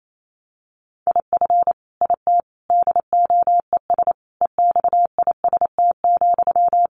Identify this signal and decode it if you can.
Morse code, heard as a keyed tone, about 28 wpm, 700 Hz. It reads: IF IT DOES EXIST,
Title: Sentences from Top 1300 Words 28wpm